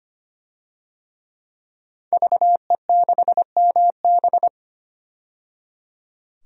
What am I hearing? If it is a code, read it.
Morse code, heard as a keyed tone, about 25 wpm, 700 Hz. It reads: VE6MB
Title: DXCC-100 Call Signs - 5 Characters - 25wpm